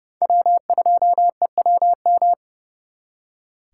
Morse code, heard as a keyed tone, about 36 wpm, 700 Hz. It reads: W2EWM
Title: Speed Racing - USA 1x3 Call Signs 30wpm